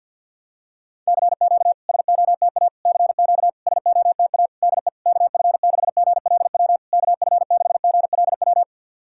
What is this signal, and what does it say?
Morse code, heard as a keyed tone, about 52 wpm, 690 Hz. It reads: CQ SOTA CQ SOTA DE KF6GLW KF6GLW
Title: SOTA QSO Practice 50wpm